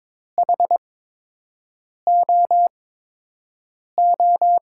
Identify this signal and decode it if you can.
Morse code, heard as a keyed tone, about 22 wpm, 700 Hz. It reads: H O O
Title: USA Call Signs - 8x Character Spacing - 22wpm